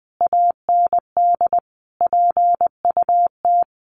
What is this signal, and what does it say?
Morse code, heard as a keyed tone, about 20 wpm, 700 Hz. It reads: AND PUT
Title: A Sound Of Thunder 20wpm